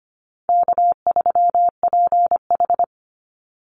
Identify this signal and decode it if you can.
Morse code, heard as a keyed tone, about 25 wpm, 700 Hz. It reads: K3PH
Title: Call Signs - 25wpm